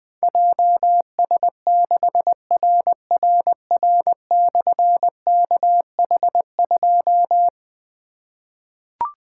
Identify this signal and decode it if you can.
Morse code, heard as a keyed tone, about 20 wpm, 700 Hz. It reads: JS6RRR/KH2
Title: Difficult Call Signs 20wpm